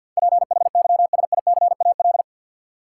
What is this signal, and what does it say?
Morse code, heard as a keyed tone, about 53 wpm, 700 Hz. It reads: PHYSICAL
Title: Sentences from Top 1400 Words - New Word Drill Alt - 50wpm